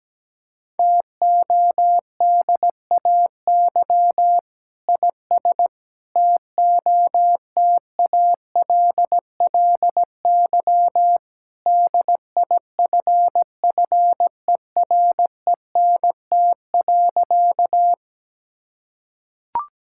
Morse code, heard as a keyed tone, about 17 wpm, 700 Hz. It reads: TODAY IS TOTALLY DIFFERENT.
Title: Sentences from Top 1600 Words 17wpm